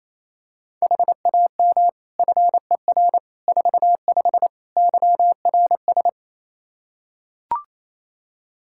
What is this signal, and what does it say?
Morse code, heard as a keyed tone, about 28 wpm, 700 Hz. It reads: HAM FER 45 YRS
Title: K1USN SST QSO Practice 28wpm